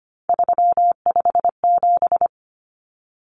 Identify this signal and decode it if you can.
Morse code, heard as a keyed tone, about 25 wpm, 700 Hz. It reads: 357
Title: Common Cut Numbers - 3-Digit 25wpm